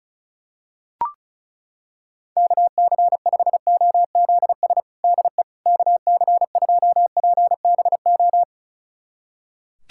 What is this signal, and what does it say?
Morse code, heard as a keyed tone, about 36 wpm, 700 Hz. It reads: KC5OZS DE KC2PBO
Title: USA Callsign DE Callsign 35wpm